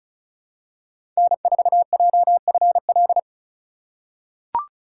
Morse code, heard as a keyed tone, about 34 wpm, 700 Hz. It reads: N4JFL
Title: USA Call Signs - 6x Character Spacing - 35wpm